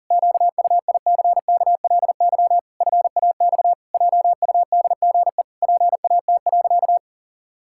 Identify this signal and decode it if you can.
Morse code, heard as a keyed tone, about 40 wpm, 700 Hz. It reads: QUICKLY FAX JUDGE PAT.
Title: Pangram Challenge v2 - 40wpm